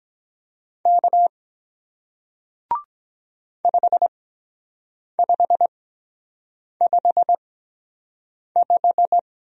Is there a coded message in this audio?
Yes: Morse code from 0.9 to 9.2 s, heard as a keyed tone, about 23 wpm, spaced as for 7 wpm, 700 Hz. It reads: K 5555